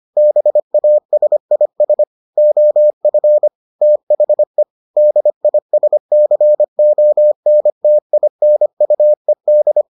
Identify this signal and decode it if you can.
Morse code, heard as a keyed tone, about 25 wpm, 600 Hz. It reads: BASIS OF THE DISCONTINUED